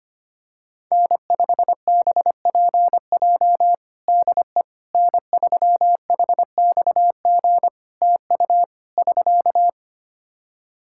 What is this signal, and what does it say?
Morse code, heard as a keyed tone, about 25 wpm, 700 Hz. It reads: N5BPJ DE N3HXG TU <SK>
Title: K1USN SST QSO Practice 25wpm